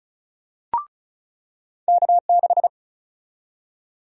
Morse code, heard as a keyed tone, about 36 wpm, 700 Hz. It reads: K6
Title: USA Call Sign Prefixes - 35wpm